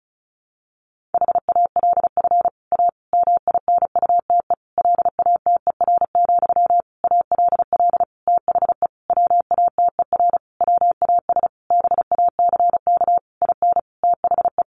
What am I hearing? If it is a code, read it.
Morse code, heard as a keyed tone, about 34 wpm, 700 Hz. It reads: HALF A MINUTE LATER, ALL THE WATER WAS BACK IN THE
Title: Sentences from Top 1100 Words 35wpm